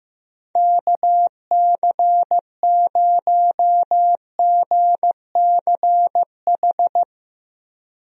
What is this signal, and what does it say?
Morse code, heard as a keyed tone, about 15 wpm, 700 Hz. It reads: KC0GCH
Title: USA Call Signs - 6 Characters - Repeated 3x - 15wpm